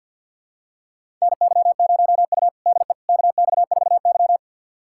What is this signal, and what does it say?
Morse code, heard as a keyed tone, about 53 wpm, 700 Hz. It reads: NY0U DE KX4Y
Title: USA Callsign DE Callsign - 4 Characters - 50wpm